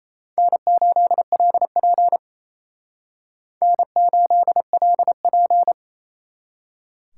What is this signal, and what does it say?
Morse code, heard as a keyed tone, about 34 wpm, 700 Hz. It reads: N8LP N8LP
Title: Speed Racing - USA 1x2 Call Signs 28wpm